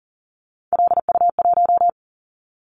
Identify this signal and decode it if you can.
Morse code, heard as a keyed tone, about 38 wpm, 700 Hz. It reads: LU1
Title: Callsign Prefixes for Top 50 DXCC - 40wpm